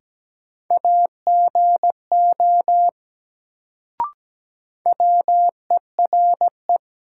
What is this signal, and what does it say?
Morse code, heard as a keyed tone, about 17 wpm, 700 Hz. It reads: AGO WERE
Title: Sets of 2 Words - New Word Drill - 17wpm